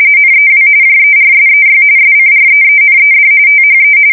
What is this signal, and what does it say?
RTTY, about 45.45 baud. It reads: NEW JERSEY SCREENED THE
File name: fsk170 / battleshipnj